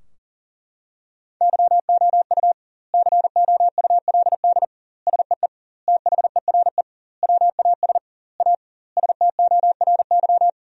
Morse code, heard as a keyed tone, about 40 wpm, 700 Hz. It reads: YOU COULD SEE THERE WAS A STORY
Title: 2x Word-Spacing - Sentences from Top 500 Words - 40wpm